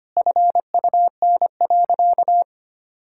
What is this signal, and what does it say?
Morse code, heard as a keyed tone, about 25 wpm, 700 Hz. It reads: FUN.
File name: practice / 25wpm / Sentences with 3-Word Phrases - 25wpm